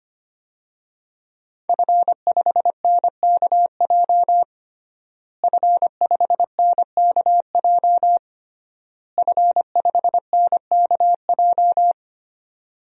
Morse code, heard as a keyed tone, about 25 wpm, 700 Hz. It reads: F5NKJ F5NKJ F5NKJ
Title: DXCC-100 Call Signs - 5 Characters - Repeated 3x - 25wpm